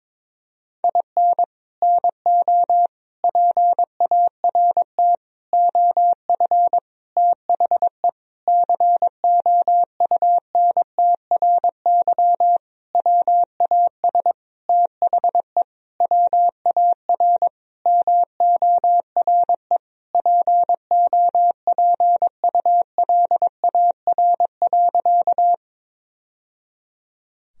Morse code, heard as a keyed tone, about 22 wpm, 700 Hz. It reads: IN NO PART OF THE COUNTRY WAS THE WAR MORE POPULAR.
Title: Sentences from Top 1000 Words - Review Part 3 - 22wpm